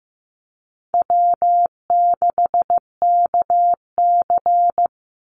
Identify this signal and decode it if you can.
Morse code, heard as a keyed tone, about 15 wpm, 700 Hz. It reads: W6KC
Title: Easy Call Signs Repeated 3x - 15wpm